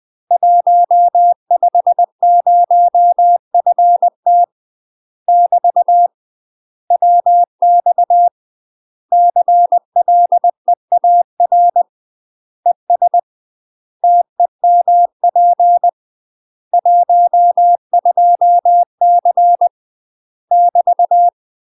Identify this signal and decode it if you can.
Morse code, heard as a keyed tone, about 20 wpm, 700 Hz. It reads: 150FT <BT> WX CLEAR ES TEMP 12C <BT>
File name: UK MASTER QSOs TRAINING / 20WPM / UK MASTER QSOs 0029